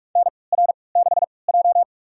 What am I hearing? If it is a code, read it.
Morse code, heard as a keyed tone, about 47 wpm, spaced as for 26 wpm, 700 Hz. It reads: NR6J
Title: USA Call Signs - 4 Characters - 3x Character Spacing - 45wpm